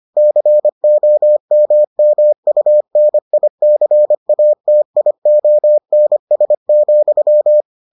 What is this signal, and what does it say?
Morse code, heard as a keyed tone, about 25 wpm, 600 Hz. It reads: COMMUNICATIONS,